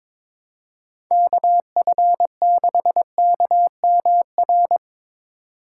Morse code, heard as a keyed tone, about 22 wpm, 700 Hz. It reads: KF6KMR
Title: USA Call Signs - 2x Character Spacing - 22wpm